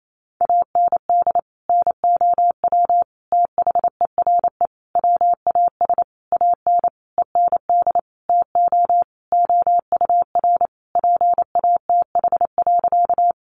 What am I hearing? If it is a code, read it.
Morse code, heard as a keyed tone, about 28 wpm, 700 Hz. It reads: AND NOW THERE WAS AN END TO OUR PATH.
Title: Sentences from Top 1400 Words 28wpm